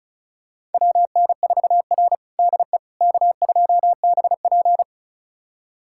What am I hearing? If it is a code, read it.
Morse code, heard as a keyed tone, about 36 wpm, 700 Hz. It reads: WN4R DE K2BP
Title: USA Callsign DE Callsign - 4 Characters - 35wpm